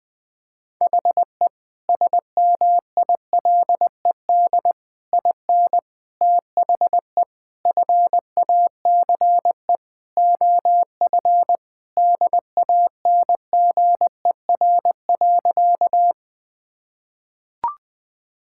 Morse code, heard as a keyed tone, about 20 wpm, 700 Hz. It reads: HE SMILED IN THE FACE OF DANGER.
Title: Sentences with 4-Word Phrases - 20wpm